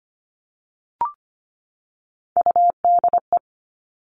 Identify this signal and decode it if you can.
Morse code, heard as a keyed tone, about 25 wpm, 700 Hz. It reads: UDE
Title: All Cut Numbers - 3-Digit 25wpm